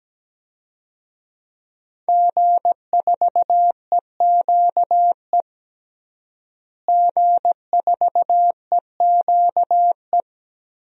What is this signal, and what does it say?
Morse code, heard as a keyed tone, about 17 wpm, 700 Hz. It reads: G4EQE G4EQE
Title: DXCC-100 Call Signs - 5 Characters - Repeated 2x - 17wpm